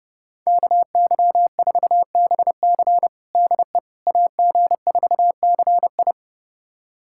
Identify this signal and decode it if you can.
Morse code, heard as a keyed tone, about 32 wpm, 700 Hz. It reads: KY4BC DE AG4CI
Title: USA Callsign DE Callsign - 5 Characters - 30wpm